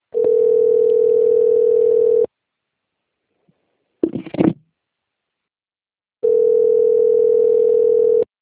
Ringback tone (North America), 440 + 480 Hz.